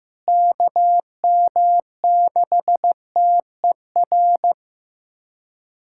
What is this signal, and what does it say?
Morse code, heard as a keyed tone, about 16 wpm, 700 Hz. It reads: KM6TER
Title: USA Callsign DE Callsign 15wpm